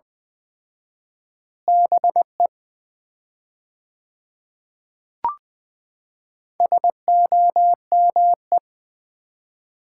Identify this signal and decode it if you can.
Morse code, heard as a keyed tone, about 20 wpm, 700 Hz. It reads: BE SOME
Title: 4x Word-Spacing - Sets of 3 Words - 20wpm